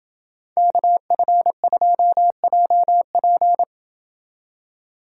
Morse code, heard as a keyed tone, about 29 wpm, 700 Hz. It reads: KF2JP